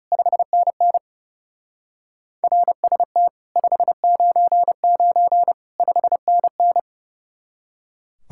Morse code, heard as a keyed tone, about 38 wpm, 700 Hz. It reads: 5NN RST 599 5NN